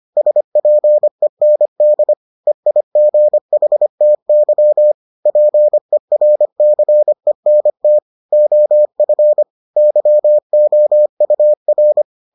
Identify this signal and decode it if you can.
Morse code, heard as a keyed tone, about 25 wpm, 600 Hz. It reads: SPEND EIGHTY PERCENT OF YOUR